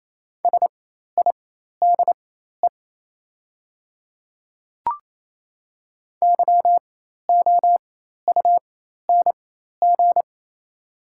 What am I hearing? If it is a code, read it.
Morse code, heard as a keyed tone, about 28 wpm, spaced as for 13 wpm, 700 Hz. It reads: SIDE YOUNG
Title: 4x Character Spacing - Top 500 Words - 28wpm